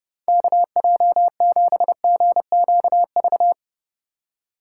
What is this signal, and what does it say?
Morse code, heard as a keyed tone, about 29 wpm, 700 Hz. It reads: KJ7GQV